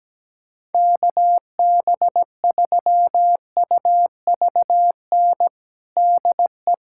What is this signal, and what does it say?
Morse code, heard as a keyed tone, about 17 wpm, 700 Hz. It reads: KB3UVN DE
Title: USA Callsign DE Callsign 17wpm